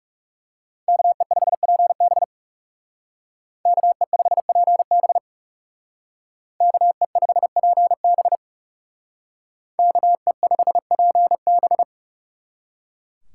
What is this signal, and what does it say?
Morse code, heard as a keyed tone, about 37 wpm, 700 Hz. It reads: KE5PB KE5PB KE5PB KE5PB
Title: Speed Racing - USA 2x2 Call Signs 30wpm